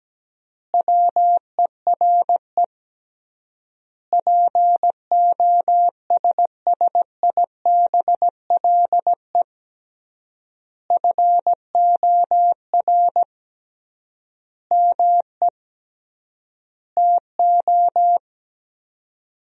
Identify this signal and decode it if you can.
Morse code, heard as a keyed tone, about 17 wpm, 700 Hz. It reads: WERE POSSIBLE FOR ME TO
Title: 3x Word-Spacing - Sentences from Top 500 Words - 17wpm